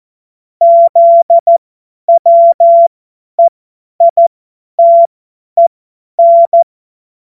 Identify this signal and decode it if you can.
Morse code, heard as a keyed tone, about 14 wpm, spaced as for 10 wpm, 680 Hz. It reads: ZWEITEN
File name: Morsekurs / Witze / Witz 30. Dezember 2025 50 F70